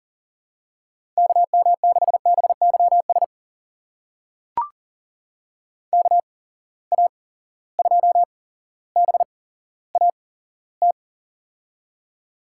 Morse code, heard as a keyed tone, about 38 wpm, spaced as for 17 wpm, 700 Hz. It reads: KM6BYS KA2BAT
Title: USA Call Signs - 6 Characters - 8x Character Spacing - 40wpm